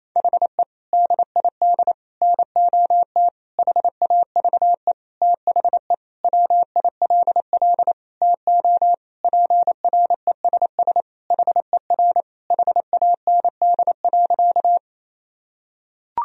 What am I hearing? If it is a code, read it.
Morse code, heard as a keyed tone, about 28 wpm, 700 Hz. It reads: HE DID NOT HAVE THE WILL TO PRESS HER HAND.
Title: Sentences from Top 1000 Words - Review Part 1 - 28wpm